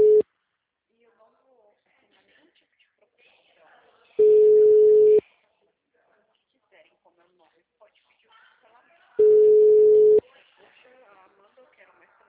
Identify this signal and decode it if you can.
Ringback tone (Europe, 425 Hz), with speech.